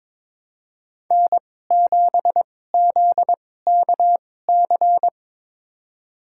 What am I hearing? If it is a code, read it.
Morse code, heard as a keyed tone, about 22 wpm, spaced as for 16 wpm, 700 Hz. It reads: N7ZKC
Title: USA Call Signs - 5 Characters - 2x Character Spacing - 22wpm